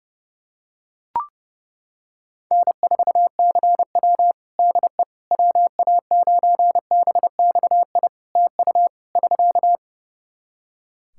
Morse code, heard as a keyed tone, about 30 wpm, 700 Hz. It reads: N4CW DE WA9BXI TU <SK>
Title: Practice QSO - Exchange 4 - Ending - 30wpm